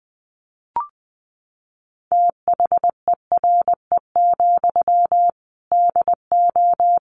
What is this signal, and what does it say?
Morse code, heard as a keyed tone, about 20 wpm, 700 Hz. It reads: THERE, DO
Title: Sentences from Top 300 Words 20wpm